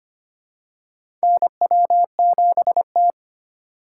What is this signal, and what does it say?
Morse code, heard as a keyed tone, about 25 wpm, 700 Hz. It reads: NW7T
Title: USA Call Signs - 2x Character Spacing - 25wpm